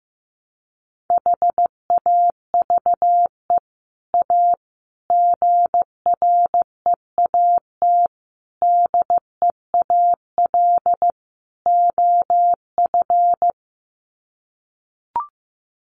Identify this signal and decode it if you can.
Morse code, heard as a keyed tone, about 15 wpm, 700 Hz. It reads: HAVE A GREAT DEAL OF
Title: Sets of 5 Words Encore2 15wpm